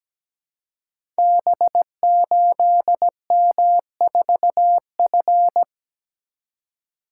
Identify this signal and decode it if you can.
Morse code, heard as a keyed tone, about 17 wpm, 700 Hz. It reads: B8M4F
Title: Code Groups - 5 Letter:Numbers - 17wpm